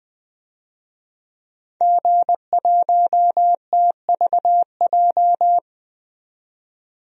Morse code, heard as a keyed tone, about 20 wpm, 700 Hz. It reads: G1TVJ
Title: DXCC-100 Call Signs - 5 Characters - 20wpm